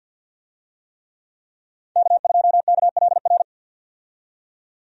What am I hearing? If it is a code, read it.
Morse code, heard as a keyed tone, about 51 wpm, 700 Hz. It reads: K2KLR
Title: USA Call Signs - 5 Characters - 50wpm